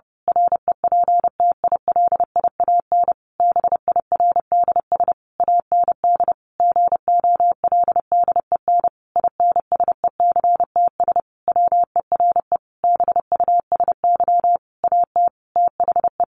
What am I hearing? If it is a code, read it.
Morse code, heard as a keyed tone, about 30 wpm, 700 Hz. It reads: REPTILIAN BIRDS AND GOLDEN INSECTS WERE BUSY AT THE
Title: A Sound Of Thunder 30wpm